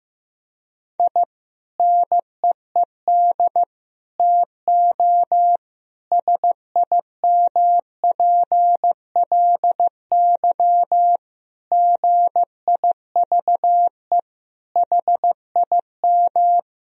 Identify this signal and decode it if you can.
Morse code, heard as a keyed tone, about 15 wpm, 700 Hz. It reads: I NEED TO SIMPLY GIVE HIM
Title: Sentences from Top 1000 Words - Review Part 4 - 15wpm